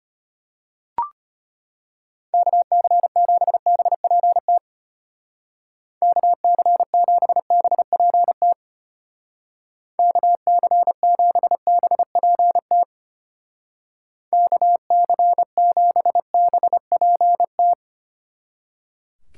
Morse code, heard as a keyed tone, about 31 wpm, 700 Hz. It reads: KC7BPT KC7BPT KC7BPT KC7BPT
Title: Speed Racing - USA Call Signs 25wpm